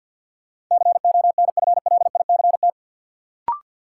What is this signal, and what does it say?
Morse code, heard as a keyed tone, about 52 wpm, 700 Hz. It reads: CONFLICT
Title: Sentences from Top 1300 Words - New Word Drill - 50wpm